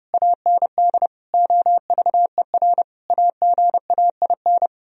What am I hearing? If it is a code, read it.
Morse code, heard as a keyed tone, about 28 wpm, 700 Hz. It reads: AND OVER AGAIN
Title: Speed-Racing - Sets of 4 Words 30wpm